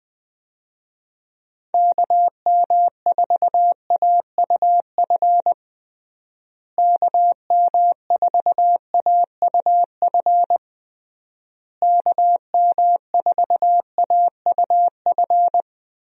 Morse code, heard as a keyed tone, about 20 wpm, 700 Hz. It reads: KM4AUF KM4AUF KM4AUF
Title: USA Call Signs - 6 Characters - Repeated 3x - 20wpm